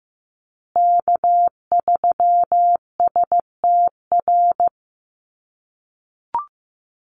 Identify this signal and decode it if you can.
Morse code, heard as a keyed tone, about 15 wpm, 700 Hz. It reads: K3STR
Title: USA Call Signs Repeated 2x 15wpm